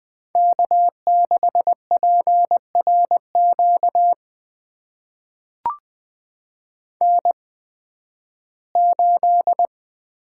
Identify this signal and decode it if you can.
Morse code, heard as a keyed tone, about 20 wpm, spaced as for 13 wpm, 700 Hz. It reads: K6PRQ N8
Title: USA Call Signs - 5 Characters - 8x Character Spacing - 20wpm